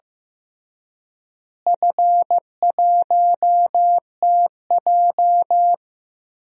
Morse code, heard as a keyed tone, about 15 wpm, 700 Hz. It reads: F1TJ
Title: DXCC-100 Call Signs - 4 Characters - Repeated 3x - 15wpm